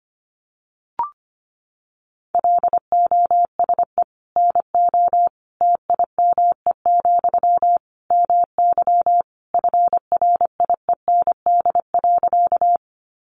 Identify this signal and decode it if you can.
Morse code, heard as a keyed tone, about 25 wpm, 700 Hz. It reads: LOSE NO TIME, MY FRIEND.
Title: Sentences from Top 1100 Words 25wpm